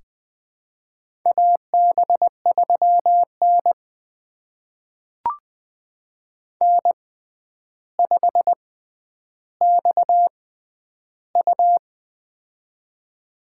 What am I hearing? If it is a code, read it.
Morse code, heard as a keyed tone, about 20 wpm, spaced as for 10 wpm, 700 Hz. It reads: AB3N N5XU
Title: USA Call Signs - 4 Characters - 6x Character Spacing - 20wpm